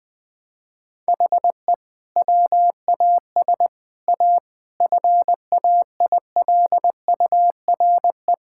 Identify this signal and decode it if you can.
Morse code, heard as a keyed tone, about 20 wpm, 700 Hz. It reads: HE WAS A FAILURE
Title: Sentences from Top 1000 Words - Review Part 6 - 20wpm